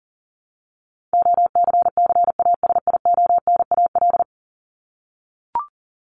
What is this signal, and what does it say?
Morse code, heard as a keyed tone, about 40 wpm, 700 Hz. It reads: OCCASIONAL
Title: LY Suffix - Top 100 Words with Base Words 40wpm